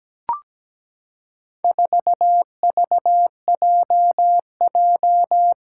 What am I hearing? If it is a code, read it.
Morse code, heard as a keyed tone, about 17 wpm, 700 Hz. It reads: 4VJJ
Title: Four Challenging Letter-Number Combinations 17wpm